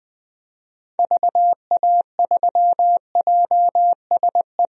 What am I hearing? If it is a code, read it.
Morse code, heard as a keyed tone, about 21 wpm, 700 Hz. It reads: VA3JSE
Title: DXCC-100 Call Signs - 6 Characters - Repeated 3x - 20wpm